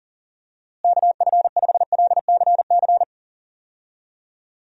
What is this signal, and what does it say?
Morse code, heard as a keyed tone, about 40 wpm, 700 Hz. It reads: KF5LCC
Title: USA Call Signs - 6 Characters - Repeated 2x - 40wpm